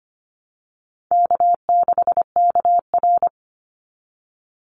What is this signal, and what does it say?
Morse code, heard as a keyed tone, about 25 wpm, 700 Hz. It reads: K6KR
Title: Easy Call Signs Repeated 2x - 25wpm